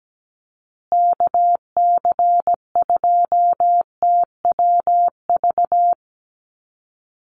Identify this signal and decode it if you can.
Morse code, heard as a keyed tone, about 17 wpm, 700 Hz. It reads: KC2TWV
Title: USA Call Signs Repeated 3x 17wpm